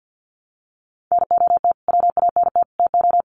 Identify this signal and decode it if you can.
Morse code, heard as a keyed tone, about 50 wpm, 700 Hz. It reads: NOT WANT TO
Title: Sets of 3 Words 50wpm